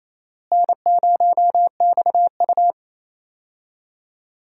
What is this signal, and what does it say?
Morse code, heard as a keyed tone, about 28 wpm, 700 Hz. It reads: N0XU
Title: USA Call Signs - 4 Characters - Repeated 2x - 28wpm